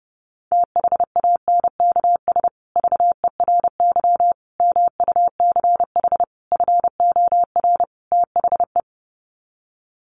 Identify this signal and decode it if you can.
Morse code, heard as a keyed tone, about 30 wpm, 700 Hz. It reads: THANKS VERY MUCH FOR THE
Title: Sets of 5 Words Encore2 30wpm